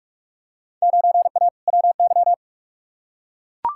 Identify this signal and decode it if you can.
Morse code, heard as a keyed tone, about 42 wpm, 700 Hz. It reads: 9A WY
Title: Field Day Exchanges Encore - 45wpm